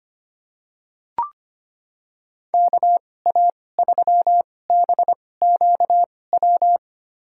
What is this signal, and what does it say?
Morse code, heard as a keyed tone, about 25 wpm, spaced as for 18 wpm, 700 Hz. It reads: KA3BQW
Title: USA Call Signs - 6 Characters - 2x Character Spacing - 25wpm